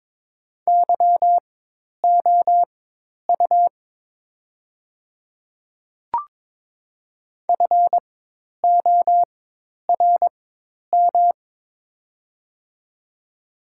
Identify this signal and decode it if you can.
Morse code, heard as a keyed tone, about 22 wpm, spaced as for 10 wpm, 700 Hz. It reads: YOU FORM
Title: 4x Character Spacing - Top 500 Words - 22wpm